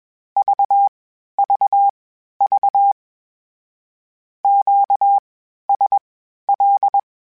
Morse code, heard as a keyed tone, about 21 wpm, spaced as for 12 wpm, 800 Hz. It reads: VVV QSL
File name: (spaced) / Q-Calls Spaced-mp3s / Q Calls Twice sp @20